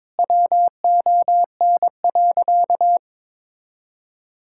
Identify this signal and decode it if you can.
Morse code, heard as a keyed tone, about 22 wpm, 700 Hz. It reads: WON.
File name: Sentences with 7-Word Phrases - 22wpm